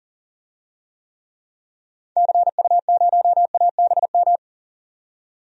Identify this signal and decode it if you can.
Morse code, heard as a keyed tone, about 40 wpm, 700 Hz. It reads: CU0ABM
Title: DXCC-100 Call Signs - 6 Characters - 40wpm